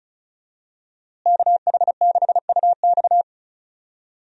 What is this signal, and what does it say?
Morse code, heard as a keyed tone, about 34 wpm, 700 Hz. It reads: KH6UX
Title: USA Call Signs - 5 Characters - Repeated 3x - 35wpm